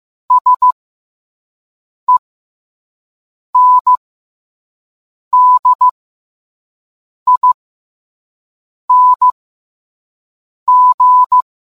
Morse code, heard as a keyed tone, about 15 wpm, spaced as for 5 wpm, 1000 Hz. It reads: SENDING